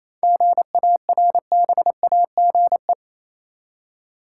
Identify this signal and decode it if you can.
Morse code, heard as a keyed tone, about 27 wpm, 700 Hz. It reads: GARBAGE
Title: Seven Letter Words - 28wpm